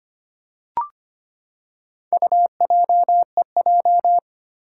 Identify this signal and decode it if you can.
Morse code, heard as a keyed tone, about 25 wpm, 700 Hz. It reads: UJEJ
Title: Four Challenging Letter-Number Combinations 25wpm